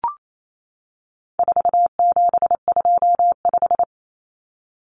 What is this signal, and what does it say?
Morse code, heard as a keyed tone, about 28 wpm, 700 Hz. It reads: 4725
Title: Numbers - 4-Digit 28wpm